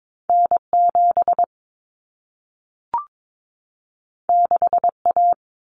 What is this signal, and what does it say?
Morse code, heard as a keyed tone, about 22 wpm, 700 Hz. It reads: N7 6A
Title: Common Cut Numbers - 2-Digit 22wpm